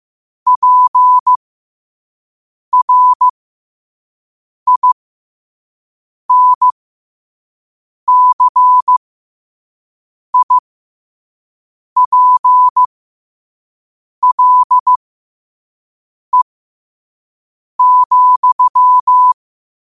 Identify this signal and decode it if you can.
Morse code, heard as a keyed tone, about 15 wpm, spaced as for 5 wpm, 1000 Hz. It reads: PRINCIPLE,
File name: qotdfiles / 5wpm